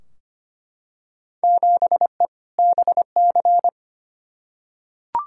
Morse code, heard as a keyed tone, about 25 wpm, 700 Hz. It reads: 7E BC
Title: Field Day Exchanges Encore - 25wpm